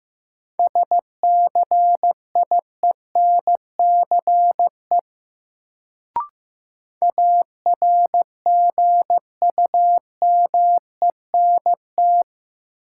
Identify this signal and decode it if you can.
Morse code, heard as a keyed tone, about 15 wpm, 700 Hz. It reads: SCIENCE ARGUMENT
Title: Sentences from Top 1400 Words - New Word Drill Alt - 15wpm